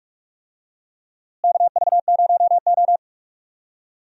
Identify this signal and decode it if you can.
Morse code, heard as a keyed tone, about 43 wpm, 700 Hz. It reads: KV0O
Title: USA Call Signs - 4 Characters - Repeated 2x - 45wpm